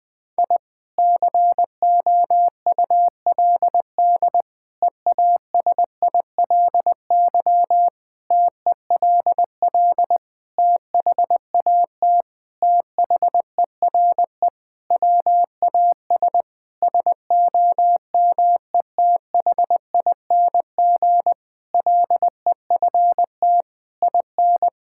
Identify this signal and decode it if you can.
Morse code, heard as a keyed tone, about 21 wpm, 700 Hz. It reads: I COULD EASILY TELL THAT THERE WAS SOMETHING LEFT IN
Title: Sentences from Top 1000 Words - Review Part 6 - 20wpm